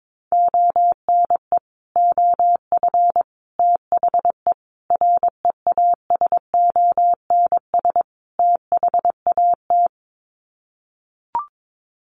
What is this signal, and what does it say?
Morse code, heard as a keyed tone, about 22 wpm, 700 Hz. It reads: ONE OF THE REASONS THAT
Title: Sets of 5 Words Encore 22wpm